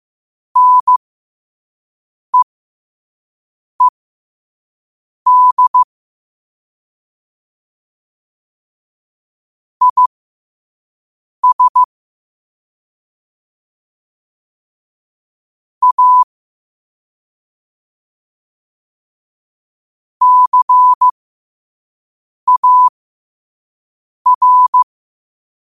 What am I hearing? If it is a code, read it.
Morse code, heard as a keyed tone, about 15 wpm, spaced as for 5 wpm, 1000 Hz. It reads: NEED IS A CAR
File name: qotdfiles / 5wpm